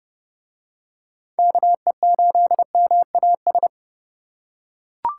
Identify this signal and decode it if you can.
Morse code, heard as a keyed tone, about 30 wpm, 700 Hz. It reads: KE8MAS